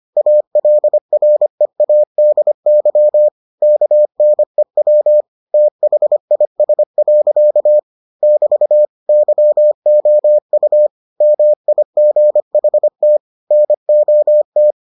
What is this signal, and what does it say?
Morse code, heard as a keyed tone, about 25 wpm, 600 Hz. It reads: ALREADY KNEW THIS. <BT> YOU MIGHT NOT